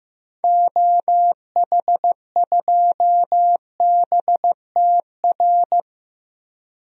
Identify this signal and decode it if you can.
Morse code, heard as a keyed tone, about 15 wpm, 700 Hz. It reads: OH2BTR
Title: DXCC-100 Call Signs - 6 Characters - 15wpm